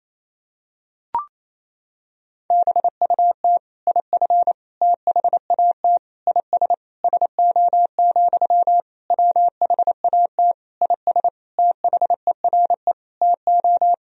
Morse code, heard as a keyed tone, about 28 wpm, 700 Hz. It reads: BUT IF THAT IS SO, WHAT IS THERE TO
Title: Sentences from Top 200 Words and Abbreviations 28wpm